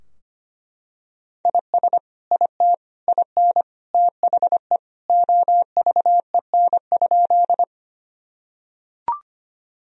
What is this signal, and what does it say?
Morse code, heard as a keyed tone, about 25 wpm, 700 Hz. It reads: IS IT IN THE OVEN?
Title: Sentences from Top 1000 Words - Review Part 4 - 25wpm